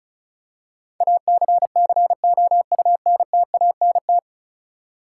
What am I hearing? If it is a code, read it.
Morse code, heard as a keyed tone, about 36 wpm, 700 Hz. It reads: ACCOUNTANT
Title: K1USN SST QSO Practice 35wpm